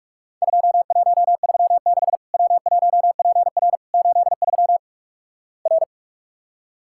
Morse code, heard as a keyed tone, about 46 wpm, 700 Hz. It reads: 2136 W1PR 83 R
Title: Worked All Europe DX Contest - QTC Exchange - 45wpm